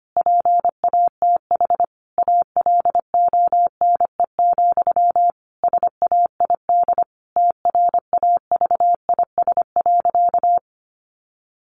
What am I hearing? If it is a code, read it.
Morse code, heard as a keyed tone, about 25 wpm, 700 Hz. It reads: PATH ALONE, SAID TRAVIS.
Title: A Sound Of Thunder 25wpm